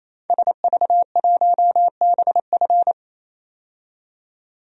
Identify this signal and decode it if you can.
Morse code, heard as a keyed tone, about 27 wpm, 700 Hz. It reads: SV1BF
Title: DXCC-100 Call Signs - 5 Characters - 28wpm